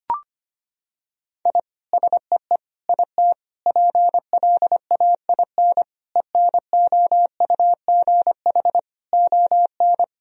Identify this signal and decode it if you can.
Morse code, heard as a keyed tone, about 25 wpm, 700 Hz. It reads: I SEE IT PLAIN ENOUGH ON
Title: Sentences from Top 1000 Words - Review Part 2 - 25wpm